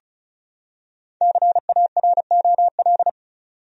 Morse code, heard as a keyed tone, about 36 wpm, 700 Hz. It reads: CAROL
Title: USA Common Names - Max 5 Letters - 35wpm